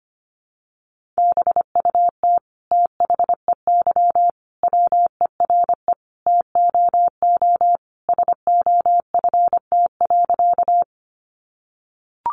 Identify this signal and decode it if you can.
Morse code, heard as a keyed tone, about 25 wpm, 700 Hz. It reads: BUT THEY WERE TOO SOFT.
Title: Sentences from Top 1500 Words 25wpm